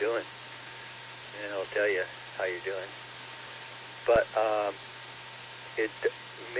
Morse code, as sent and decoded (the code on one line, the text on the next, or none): none